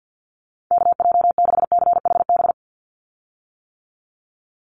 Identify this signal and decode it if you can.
Morse code, heard as a keyed tone, about 48 wpm, 700 Hz. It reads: KJ6CHB
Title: USA Call Signs 50wpm